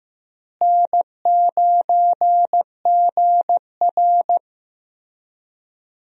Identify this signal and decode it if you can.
Morse code, heard as a keyed tone, about 15 wpm, 700 Hz. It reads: N9GR
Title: USA Call Signs - 4 Characters - Repeated 3x - 15wpm